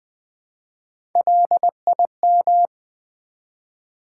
Morse code, heard as a keyed tone, about 20 wpm, 700 Hz. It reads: LIM
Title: Build A Word - 1 Letter At A Time - 3 to 5-Letter Words - 20wpm